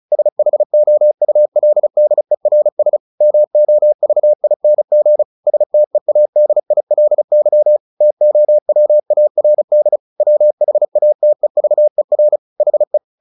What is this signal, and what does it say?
Morse code, heard as a keyed tone, about 34 wpm, 600 Hz. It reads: SHOULDERS MOVING STEADILY TOWARD WHATEVER HE
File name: the old man and the sea / the old man and the sea 1 35wpm